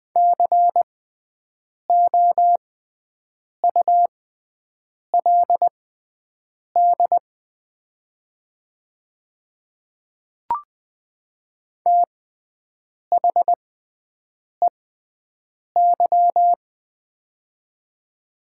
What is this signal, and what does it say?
Morse code, heard as a keyed tone, about 20 wpm, spaced as for 7 wpm, 700 Hz. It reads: COULD THEY
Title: 6x Character Spacing - Top 100 Words - 20wpm